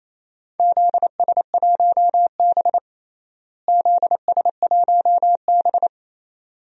Morse code, heard as a keyed tone, about 28 wpm, 700 Hz. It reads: ZS1B ZS1B
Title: DXCC-100 Call Signs - 4 Characters - Repeated 2x - 28wpm